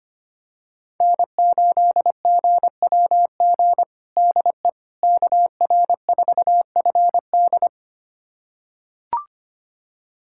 Morse code, heard as a keyed tone, about 25 wpm, 700 Hz. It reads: N8GWG DE KR4FD
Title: USA Callsign DE Callsign - 5 Characters - 25wpm